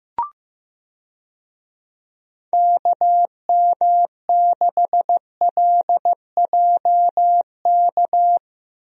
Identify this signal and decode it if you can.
Morse code, heard as a keyed tone, about 15 wpm, 700 Hz. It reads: KM6LJK
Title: USA Call Signs - 6 Characters - Repeated 2x - 15wpm